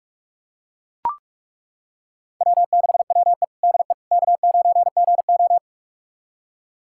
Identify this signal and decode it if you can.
Morse code, heard as a keyed tone, about 42 wpm, 700 Hz. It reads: W6WE DE K9GO
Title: USA Callsign DE Callsign - 4 Characters - 45wpm